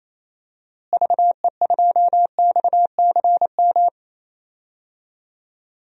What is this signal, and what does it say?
Morse code, heard as a keyed tone, about 28 wpm, 700 Hz. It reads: VE2XCM
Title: DXCC-100 Call Signs - 6 Characters - Repeated 2x - 28wpm